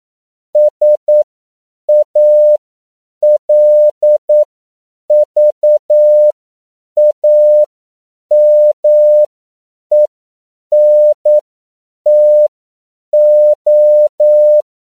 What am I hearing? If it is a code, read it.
Morse code, heard as a keyed tone, about 9 wpm, spaced as for 7 wpm, 600 Hz. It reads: SALVAMENTO
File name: CW / 10PPM